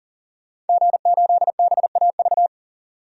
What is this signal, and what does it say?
Morse code, heard as a keyed tone, about 40 wpm, 700 Hz. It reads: G8BAV
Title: DXCC-100 Call Signs - 5 Characters - 40wpm